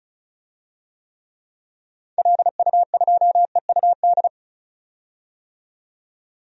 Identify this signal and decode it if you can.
Morse code, heard as a keyed tone, about 35 wpm, 700 Hz. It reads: LU2EUD
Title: DXCC-100 Call Signs - 6 Characters - 35wpm